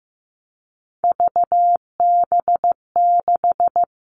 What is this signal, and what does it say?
Morse code, heard as a keyed tone, about 15 wpm, 700 Hz. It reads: VB6
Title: Three Challenging Letter-Number Combinations 15wpm